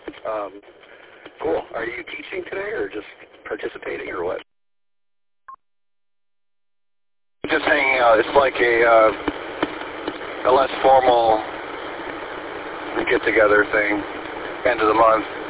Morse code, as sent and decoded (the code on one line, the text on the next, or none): none